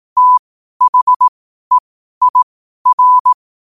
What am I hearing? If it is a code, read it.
Morse code, heard as a keyed tone, about 18 wpm, spaced as for 12 wpm, 1000 Hz. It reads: THEIR
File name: qotdfiles / 12wpm